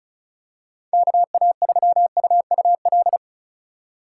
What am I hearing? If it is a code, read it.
Morse code, heard as a keyed tone, about 36 wpm, 690 Hz. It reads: KA3UUL
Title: USA Call Signs - 6 Characters - 2x Character Spacing - 35wpm